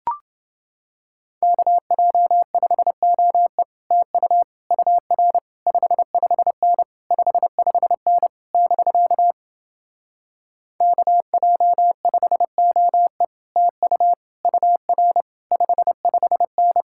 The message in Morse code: KJ5OE TU UR 55N 55N <BK> KJ5OE TU UR 55N

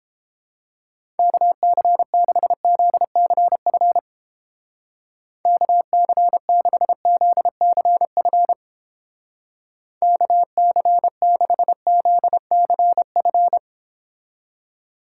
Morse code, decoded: KC6ZCF KC6ZCF KC6ZCF

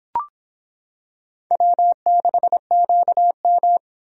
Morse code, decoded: W6QM

